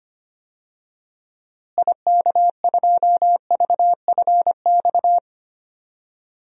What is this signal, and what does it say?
Morse code, heard as a keyed tone, about 25 wpm, 700 Hz. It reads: IK2VFX
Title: DXCC-100 Call Signs - 6 Characters - Repeated 2x - 25wpm